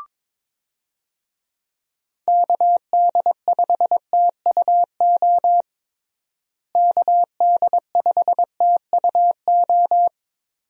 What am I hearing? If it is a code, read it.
Morse code, heard as a keyed tone, about 22 wpm, 700 Hz. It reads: KD5TUO KD5TUO
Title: USA Call Signs - 6 Characters - Repeated 2x - 22wpm